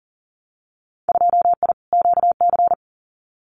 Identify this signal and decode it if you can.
Morse code, heard as a keyed tone, about 42 wpm, 700 Hz. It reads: 2I QC
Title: Winter Field Day Exchanges 40wpm